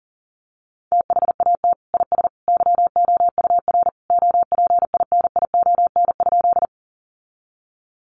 Morse code, heard as a keyed tone, about 40 wpm, 700 Hz. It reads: THAT IS YOUR OPINION?
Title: Sentences from Top 1200 Words 40wpm